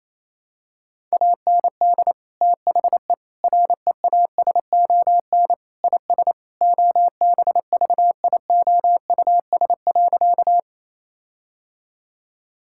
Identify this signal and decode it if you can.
Morse code, heard as a keyed tone, about 28 wpm, 700 Hz. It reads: AND THE REASON IS OBVIOUS.
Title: Sentences from Top 1600 Words 28wpm